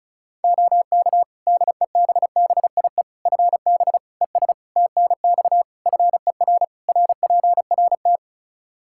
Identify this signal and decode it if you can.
Morse code, heard as a keyed tone, about 36 wpm, 700 Hz. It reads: OK DEBBIE FB ES TNX FER RPRT
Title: Practice QSO - Exchange 2 - Info - 35wpm